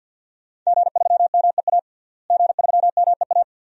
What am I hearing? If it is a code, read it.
Morse code, heard as a keyed tone, about 52 wpm, 700 Hz. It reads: G3MEA G3MEA
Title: DXCC-100 Call Signs - 5 Characters - Repeated 2x - 50wpm